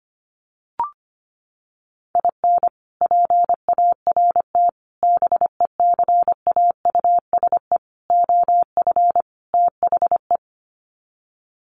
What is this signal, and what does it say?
Morse code, heard as a keyed tone, about 25 wpm, 700 Hz. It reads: IN PART BECAUSE OF THE
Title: Sets of 5 Words Encore2 25wpm